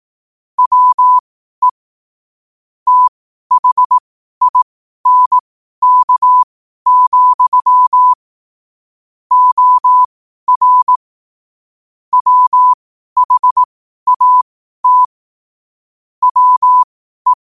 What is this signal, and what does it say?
Morse code, heard as a keyed tone, about 18 wpm, spaced as for 12 wpm, 1000 Hz. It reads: WE THINK, OR WHAT WE
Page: New quotes every day in morse code at 12 Words per minute.